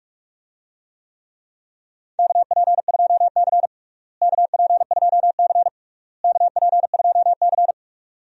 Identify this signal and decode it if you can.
Morse code, heard as a keyed tone, about 41 wpm, 700 Hz. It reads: KP2C KP2C KP2C
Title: DXCC-100 Call Signs - 4 Characters - Repeated 3x - 45wpm